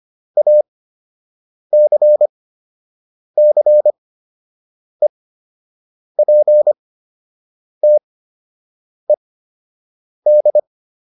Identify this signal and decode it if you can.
Morse code, heard as a keyed tone, about 25 wpm, spaced as for 7 wpm, 600 Hz. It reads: ACCEPTED